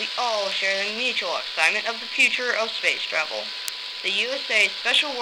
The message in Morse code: none